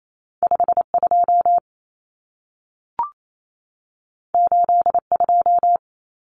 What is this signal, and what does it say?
Morse code, heard as a keyed tone, about 28 wpm, 700 Hz. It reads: 52 82
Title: Common Cut Numbers - 2-Digit 28wpm